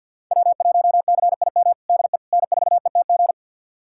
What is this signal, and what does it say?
Morse code, heard as a keyed tone, about 51 wpm, 700 Hz. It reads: W1CIM DE N4ETG
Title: USA Callsign DE Callsign - 5 Characters - 50wpm